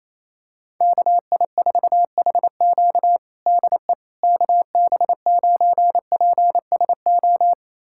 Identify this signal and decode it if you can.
Morse code, heard as a keyed tone, about 28 wpm, 700 Hz. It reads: KI4HQ DE KB9PSO
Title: USA Callsign DE Callsign 28wpm